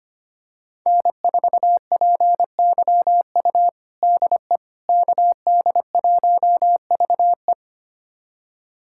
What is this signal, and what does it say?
Morse code, heard as a keyed tone, about 25 wpm, 700 Hz. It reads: N4PYU DE KD1VE
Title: USA Callsign DE Callsign - 5 Characters - 25wpm